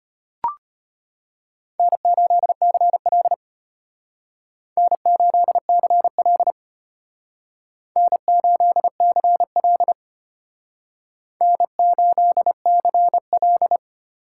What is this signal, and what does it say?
Morse code, heard as a keyed tone, about 33 wpm, 700 Hz. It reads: N8CL N8CL N8CL N8CL